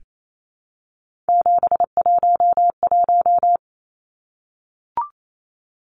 Morse code, heard as a keyed tone, about 28 wpm, 700 Hz. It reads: 711
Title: Numbers - 3-Digit 28wpm